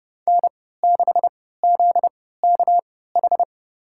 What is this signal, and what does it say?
Morse code, heard as a keyed tone, about 32 wpm, spaced as for 17 wpm, 700 Hz. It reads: N6ZKH
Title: USA Call Signs - 3x Character Spacing - 30wpm